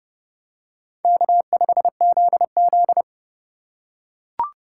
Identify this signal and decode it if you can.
Morse code, heard as a keyed tone, about 28 wpm, 700 Hz. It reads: K5ZZ